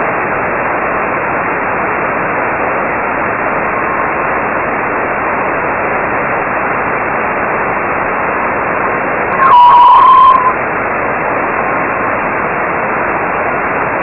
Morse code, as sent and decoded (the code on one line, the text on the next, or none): none